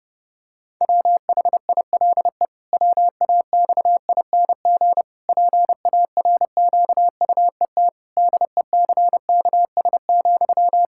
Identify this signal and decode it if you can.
Morse code, heard as a keyed tone, about 30 wpm, 700 Hz. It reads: WHILE WAXING PARQUET DECKS,
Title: Pangram Challenge v2 - 30wpm